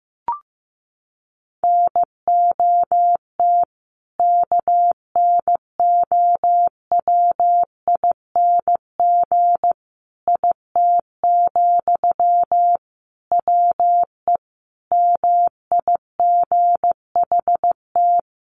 Morse code, heard as a keyed tone, about 15 wpm, 700 Hz. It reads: NOT KNOWING IT, WE MIGHT
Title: A Sound Of Thunder 15wpm